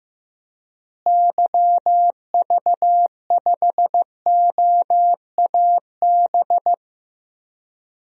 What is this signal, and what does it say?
Morse code, heard as a keyed tone, about 15 wpm, 700 Hz. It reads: YV5OAB
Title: DXCC-100 Call Signs - 6 Characters - Repeated 3x - 15wpm